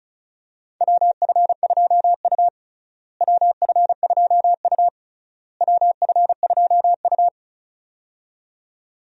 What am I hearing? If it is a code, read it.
Morse code, heard as a keyed tone, about 36 wpm, 700 Hz. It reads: WF2U WF2U WF2U
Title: USA Call Signs - 4 Characters - Repeated 3x - 35wpm